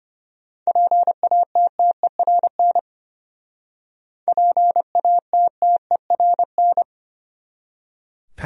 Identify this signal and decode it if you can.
Morse code, heard as a keyed tone, about 31 wpm, 700 Hz. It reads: PATTERN PATTERN
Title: Speed-Racing Top 500 Words 25wpm